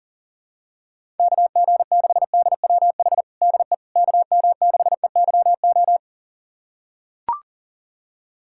Morse code, heard as a keyed tone, about 40 wpm, 700 Hz. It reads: KG6DWH DE KM6EYO
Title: USA Callsign DE Callsign - 6 Characters - 40wpm